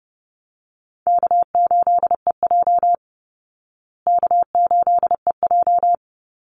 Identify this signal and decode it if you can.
Morse code, heard as a keyed tone, about 30 wpm, 700 Hz. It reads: K8EJ K8EJ
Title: Call Signs Repeated 2x - 30wpm